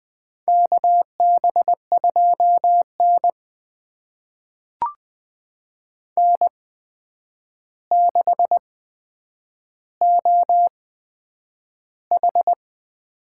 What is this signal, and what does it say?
Morse code, heard as a keyed tone, about 20 wpm, spaced as for 9 wpm, 700 Hz. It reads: KB2N N6OH
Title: USA Call Signs - 4 Characters - 8x Character Spacing - 20wpm